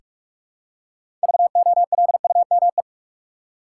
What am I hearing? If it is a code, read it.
Morse code, heard as a keyed tone, about 46 wpm, 700 Hz. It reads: VOLUME
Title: Six Letter Words - 45wpm